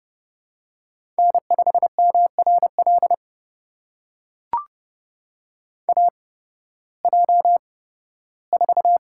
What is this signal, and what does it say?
Morse code, heard as a keyed tone, about 29 wpm, spaced as for 16 wpm, 700 Hz. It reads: N5MRL AJ4
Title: USA Call Signs - 5 Characters - 8x Character Spacing - 30wpm